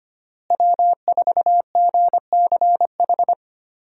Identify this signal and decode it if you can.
Morse code, heard as a keyed tone, about 25 wpm, 700 Hz. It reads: W4GCH
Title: USA Call Signs - 5 Characters - Repeated 3x - 25wpm